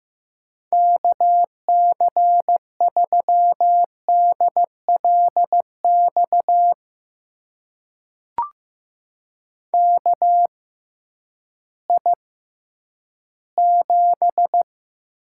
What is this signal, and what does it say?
Morse code, heard as a keyed tone, about 15 wpm, spaced as for 10 wpm, 700 Hz. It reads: KC3DLX KI7
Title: USA Call Signs - 6 Characters - 6x Character Spacing - 15wpm